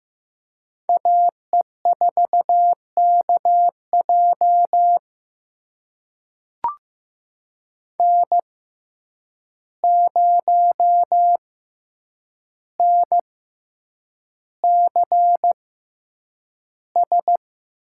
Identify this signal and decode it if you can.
Morse code, heard as a keyed tone, about 15 wpm, spaced as for 8 wpm, 700 Hz. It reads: AE4KJ N0NCS
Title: USA Call Signs - 5 Characters - 6x Character Spacing - 15wpm